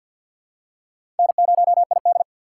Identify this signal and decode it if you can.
Morse code, heard as a keyed tone, about 47 wpm, 700 Hz. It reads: N0ID
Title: USA Call Signs - 4 Characters - 2x Character Spacing - 50wpm